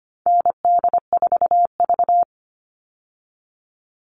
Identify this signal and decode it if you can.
Morse code, heard as a keyed tone, about 25 wpm, 700 Hz. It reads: ND4V
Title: All Cut Numbers - 4-Digit 25wpm